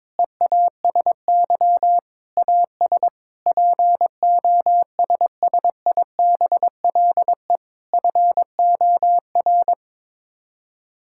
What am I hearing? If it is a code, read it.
Morse code, heard as a keyed tone, about 22 wpm, 700 Hz. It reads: EASY AS POSSIBLE FOR
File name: practice / 22wpm / Sets of 8 Words - 22wpm